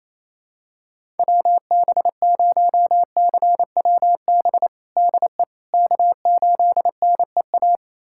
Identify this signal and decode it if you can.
Morse code, heard as a keyed tone, about 28 wpm, 700 Hz. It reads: WB0CWB DE K8NEA
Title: USA Callsign DE Callsign 28wpm